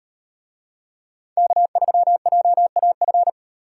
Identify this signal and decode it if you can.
Morse code, heard as a keyed tone, about 38 wpm, 700 Hz. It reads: K3JAF